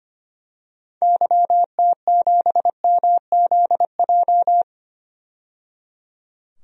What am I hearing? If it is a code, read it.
Morse code, heard as a keyed tone, about 25 wpm, 700 Hz. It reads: YT7MZJ
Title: DXCC-100 Call Signs - 6 Characters - 25wpm